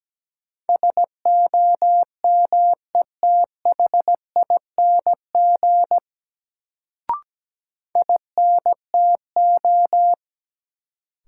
Morse code, heard as a keyed tone, about 17 wpm, 700 Hz. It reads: SOMETHING INTO
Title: Sentences from Top 600 Words - New Word Drill - 17wpm